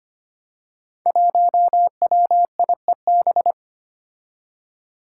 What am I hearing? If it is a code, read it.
Morse code, heard as a keyed tone, about 25 wpm, 700 Hz. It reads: 1WIEB
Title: Five Challenging Letter-Number Combinations 25wpm